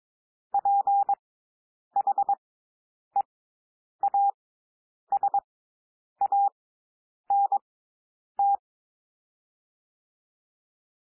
Morse code, heard as a keyed tone, about 22 wpm, spaced as for 9 wpm, 800 Hz. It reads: PHEASANT
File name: SOUND WORDS 22 wpm